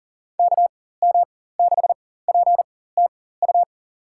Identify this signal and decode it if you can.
Morse code, heard as a keyed tone, about 40 wpm, spaced as for 19 wpm, 700 Hz. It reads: KM6PTU
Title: USA Call Signs - 6 Characters - 4x Character Spacing - 40wpm